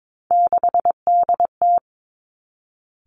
Morse code, heard as a keyed tone, about 22 wpm, 700 Hz. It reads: 6DT
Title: Three Challenging Letter-Number Combinations 22wpm